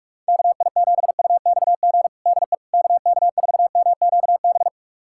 Morse code, heard as a keyed tone, about 41 wpm, 700 Hz. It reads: KI7UXG DE KK4MQB
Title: USA Callsign DE Callsign - 6 Characters - 45wpm